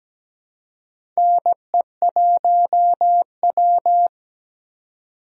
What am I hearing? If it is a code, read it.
Morse code, heard as a keyed tone, about 17 wpm, 700 Hz. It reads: NE1W
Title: USA Call Signs - 4 Characters - 4x Character Spacing - 17wpm